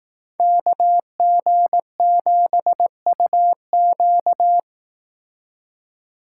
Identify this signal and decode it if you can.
Morse code, heard as a keyed tone, about 18 wpm, 700 Hz. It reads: KG7UQ
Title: Speed Racing - USA 2x2 Call Signs 15wpm